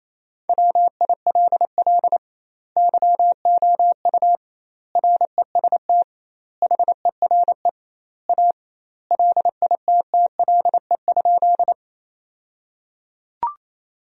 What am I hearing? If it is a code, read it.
Morse code, heard as a keyed tone, about 28 wpm, 700 Hz. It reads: WILL YOU REST HERE A LITTLE?
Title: 2x Word-Spacing - Sentences from Top 500 Words - 28wpm